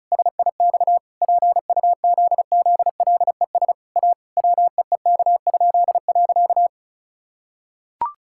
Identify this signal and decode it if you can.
Morse code, heard as a keyed tone, about 34 wpm, 700 Hz. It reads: SIX PUZZLES A WEEK?.
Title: Pangram Challenge v2 - 35wpm